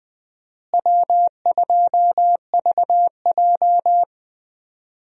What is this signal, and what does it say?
Morse code, heard as a keyed tone, about 20 wpm, 700 Hz. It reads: W2VJ
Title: USA Call Signs - 4 Characters - Repeated 3x - 20wpm